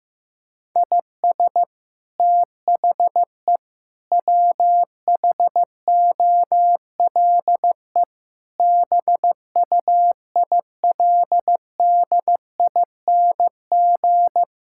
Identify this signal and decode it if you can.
Morse code, heard as a keyed tone, about 15 wpm, 700 Hz. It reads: IS THE WHOLE BUILDING
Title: Sentences from Top 1000 Words - Review Part 5 - 15wpm